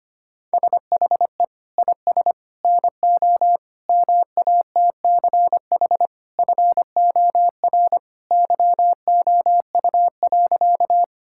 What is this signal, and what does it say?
Morse code, heard as a keyed tone, about 25 wpm, 700 Hz. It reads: SHE IS NO MATCH FOR YOU.
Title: Sentences from Top 1600 Words 25wpm